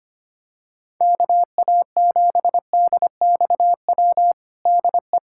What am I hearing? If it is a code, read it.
Morse code, heard as a keyed tone, about 25 wpm, 700 Hz. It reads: KA7DXW DE
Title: USA Callsign DE Callsign 25wpm